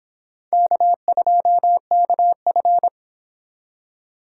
Morse code, heard as a keyed tone, about 26 wpm, 700 Hz. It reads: K2KF